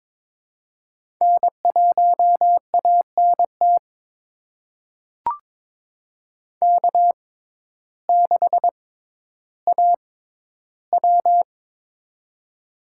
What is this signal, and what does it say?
Morse code, heard as a keyed tone, about 22 wpm, spaced as for 12 wpm, 700 Hz. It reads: N1ANT K6AW
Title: USA Call Signs - 6x Character Spacing - 22wpm